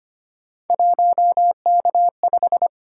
Morse code, heard as a keyed tone, about 25 wpm, 700 Hz. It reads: 1K5
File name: Three Letter-Number Combinations - 25wpm